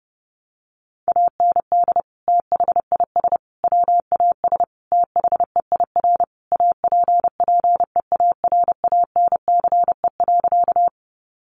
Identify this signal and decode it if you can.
Morse code, heard as a keyed tone, about 30 wpm, 700 Hz. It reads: AND THIS WAS THEIR APPEARANCE.
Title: Sentences from Top 1100 Words 30wpm